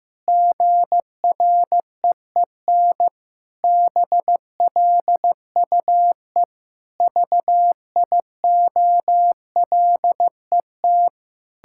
Morse code, heard as a keyed tone, about 15 wpm, 700 Hz. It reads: GREEN BLUE VIOLET
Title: Pangram Challenge v2 - 15wpm